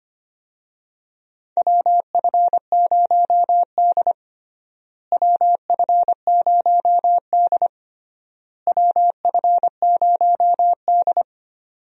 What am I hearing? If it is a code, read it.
Morse code, heard as a keyed tone, about 25 wpm, 700 Hz. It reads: WF0D WF0D WF0D
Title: USA Call Signs - 4 Characters - Repeated 3x - 25wpm